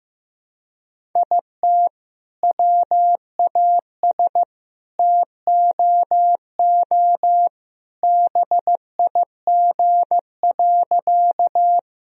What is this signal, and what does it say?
Morse code, heard as a keyed tone, about 15 wpm, 700 Hz. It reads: IT WAS TOO BIG.
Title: Sentences from Top 1000 Words - Review Part 1 - 15wpm